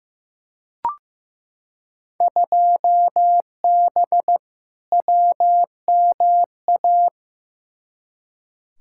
Morse code, heard as a keyed tone, about 15 wpm, 700 Hz. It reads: 2B WMA
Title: Field Day Exchanges Encore - 15wpm